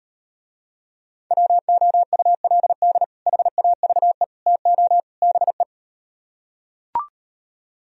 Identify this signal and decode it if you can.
Morse code, heard as a keyed tone, about 38 wpm, 700 Hz. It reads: WOULD HAVE TO BE